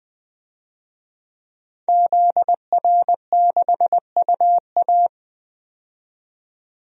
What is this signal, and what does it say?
Morse code, heard as a keyed tone, about 20 wpm, 700 Hz. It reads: ZR6UA
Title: DXCC-100 Call Signs - 5 Characters - 20wpm